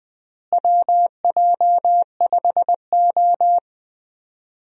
Morse code, heard as a keyed tone, about 20 wpm, 700 Hz. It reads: WJ5O
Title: USA Call Signs - 4 Characters - Repeated 3x - 20wpm